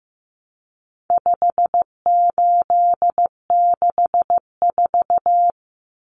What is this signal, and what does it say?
Morse code, heard as a keyed tone, about 15 wpm, 700 Hz. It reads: 5864
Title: Numbers - 4-Digit 15wpm